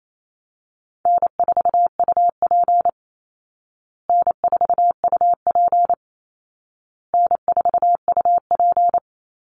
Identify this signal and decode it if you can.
Morse code, heard as a keyed tone, about 28 wpm, 700 Hz. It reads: N4UP N4UP N4UP
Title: Easy Call Signs Repeated 3x - 28wpm